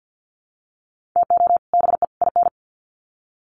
Morse code, heard as a keyed tone, about 50 wpm, 700 Hz. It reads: TO BE IN
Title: Sets of 3 Words 50wpm